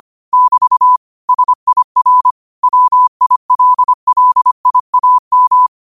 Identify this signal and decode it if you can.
Morse code, heard as a keyed tone, about 25 wpm, 1000 Hz. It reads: <BT> SIR WILLIAM
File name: 25wpm